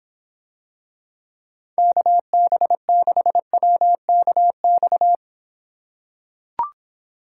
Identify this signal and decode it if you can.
Morse code, heard as a keyed tone, about 26 wpm, 700 Hz. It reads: KB6WKX